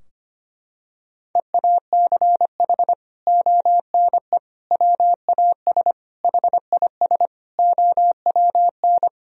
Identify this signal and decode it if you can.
Morse code, heard as a keyed tone, about 25 wpm, 700 Hz. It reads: EACH ONE WAS HIS OWN
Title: Sentences from Top 1000 Words - Review Part 4 - 25wpm